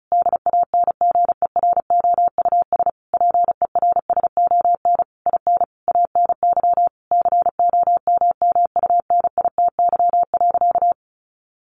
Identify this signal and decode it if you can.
Morse code, heard as a keyed tone, about 34 wpm, 700 Hz. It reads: DANGEROUS PERSON IN ANY COMMUNITY.
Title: Sentences from Top 1200 Words 35wpm